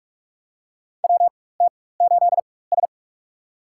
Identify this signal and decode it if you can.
Morse code, heard as a keyed tone, about 45 wpm, spaced as for 21 wpm, 700 Hz. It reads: WT8S
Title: USA Call Signs - 4 Characters - 4x Character Spacing - 45wpm